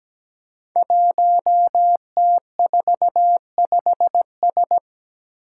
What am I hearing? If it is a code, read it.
Morse code, heard as a keyed tone, about 17 wpm, 700 Hz. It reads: 1T45S
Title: Five Challenging Letter-Number Combinations 17wpm